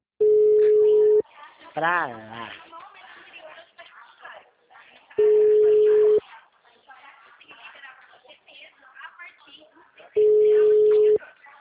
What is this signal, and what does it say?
Ringback tone (Europe, 425 Hz), with speech.